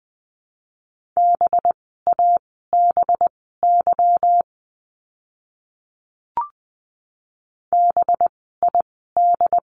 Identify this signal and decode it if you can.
Morse code, heard as a keyed tone, about 20 wpm, spaced as for 14 wpm, 700 Hz. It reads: BABY BID
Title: 185 Lesson 30f - Letter b - Words 20wpm